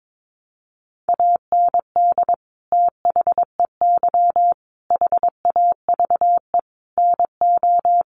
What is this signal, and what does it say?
Morse code, heard as a keyed tone, about 22 wpm, 700 Hz. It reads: AND THEY HAVE NO
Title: Sentences from Top 1400 Words 22wpm